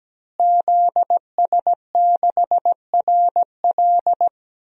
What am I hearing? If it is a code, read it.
Morse code, heard as a keyed tone, about 17 wpm, 700 Hz. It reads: ZS6RL
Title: DXCC-100 Call Signs - 5 Characters - Repeated 3x - 17wpm